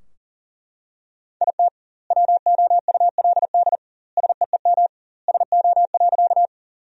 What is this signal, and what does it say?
Morse code, heard as a keyed tone, about 40 wpm, 700 Hz. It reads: IT WOULD SEEM SO.
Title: 2x Word-Spacing - Sentences from Top 300 Words - 40wpm